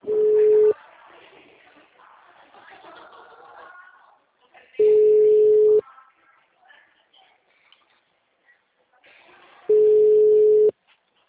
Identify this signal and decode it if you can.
Ringback tone (Europe, 425 Hz), with speech.
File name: out- -2006- - - .83094